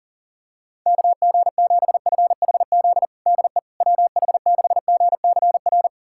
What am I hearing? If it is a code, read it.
Morse code, heard as a keyed tone, about 40 wpm, 700 Hz. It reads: KG7FHZ DE WH6GCR
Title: USA Callsign DE Callsign - 6 Characters - 40wpm